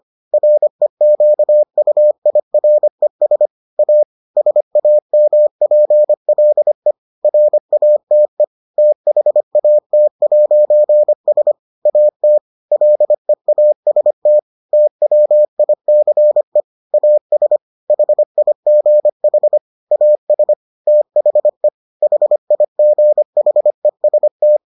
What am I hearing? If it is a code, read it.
Morse code, heard as a keyed tone, about 25 wpm, 600 Hz. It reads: REQUIRES A SAMPLE RATE THAT'S AT LEAST TWICE AS HIGH AS THE HIGHEST